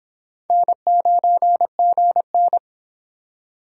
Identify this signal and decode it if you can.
Morse code, heard as a keyed tone, about 25 wpm, 700 Hz. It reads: N9GN